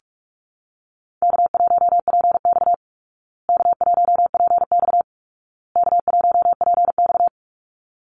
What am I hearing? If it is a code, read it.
Morse code, heard as a keyed tone, about 44 wpm, 700 Hz. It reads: K1PX K1PX K1PX
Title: Easy Call Signs Repeated 3x - 45wpm